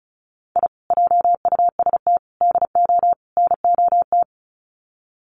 Morse code, heard as a keyed tone, about 34 wpm, 700 Hz. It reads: I JUST DO NOT
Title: Sets of 4 Words 35wpm